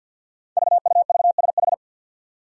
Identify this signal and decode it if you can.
Morse code, heard as a keyed tone, about 51 wpm, 700 Hz. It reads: VUVSH
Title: Five Challenging Letter-Number Combinations 50wpm